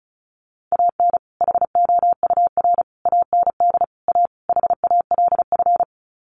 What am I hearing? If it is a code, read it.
Morse code, heard as a keyed tone, about 34 wpm, 700 Hz. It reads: AN HOUR AND A HALF
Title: Sets of 5 Words 35wpm